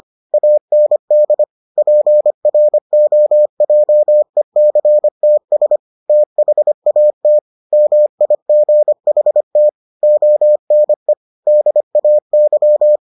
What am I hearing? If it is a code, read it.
Morse code, heard as a keyed tone, about 25 wpm, 600 Hz. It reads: AND PROJECTS THAT MIGHT ONE DAY